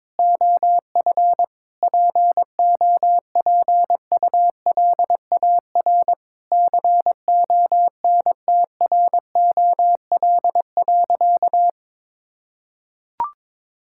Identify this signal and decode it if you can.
Morse code, heard as a keyed tone, about 22 wpm, 700 Hz. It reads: OF POPULAR CONTROL.
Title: Sentences from Top 1000 Words - Review Part 5 - 22wpm